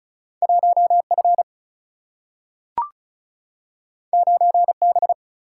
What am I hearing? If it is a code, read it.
Morse code, heard as a keyed tone, about 36 wpm, 700 Hz. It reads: 1F 9B
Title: Single-Letter Single-Number 35wpm